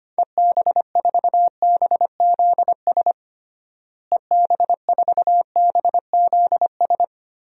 Morse code, heard as a keyed tone, about 25 wpm, 700 Hz. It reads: EB4BZS EB4BZS
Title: DXCC-100 Call Signs - 6 Characters - Repeated 3x - 25wpm